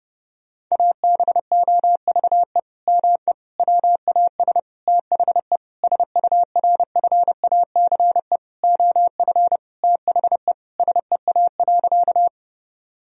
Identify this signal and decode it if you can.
Morse code, heard as a keyed tone, about 30 wpm, 700 Hz. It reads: ABOVE ME WAS THE SURFACE OF THE SEA.
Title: Sentences from Top 1000 Words - Review Part 3 - 30wpm